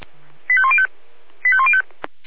Five-tone selective call (ZVEI), heard as quick runs of five tones, tones of about 70 ms. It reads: 85105 85105